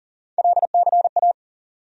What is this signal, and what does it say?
Morse code, heard as a keyed tone, about 40 wpm, 700 Hz. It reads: LCA
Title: USA Call Sign Postfixes - 3 Letters 40wpm